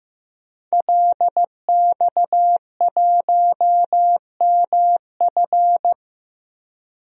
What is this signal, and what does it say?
Morse code, heard as a keyed tone, about 15 wpm, 700 Hz. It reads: LX1MF
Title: DXCC-100 Call Signs - 5 Characters - Repeated 2x - 15wpm